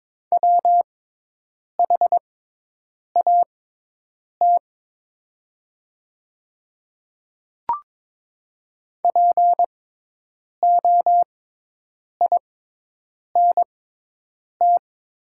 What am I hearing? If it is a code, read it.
Morse code, heard as a keyed tone, about 22 wpm, spaced as for 8 wpm, 700 Hz. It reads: WHAT POINT
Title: 6x Character Spacing - Top 300 Words - 22wpm